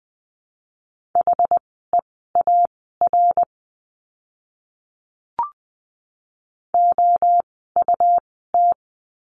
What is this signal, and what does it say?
Morse code, heard as a keyed tone, about 20 wpm, spaced as for 14 wpm, 700 Hz. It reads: HEAR OUT
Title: 206 Lesson 33f - Letter j - Words 20wpm